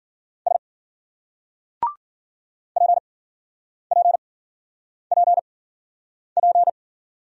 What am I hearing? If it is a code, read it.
Morse code, heard as a keyed tone, about 50 wpm, spaced as for 9 wpm, 700 Hz. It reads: S PPPP